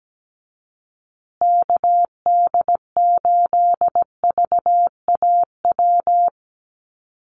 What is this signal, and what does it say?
Morse code, heard as a keyed tone, about 17 wpm, 700 Hz. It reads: KD8VAW
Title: USA Call Signs Repeated 3x 17wpm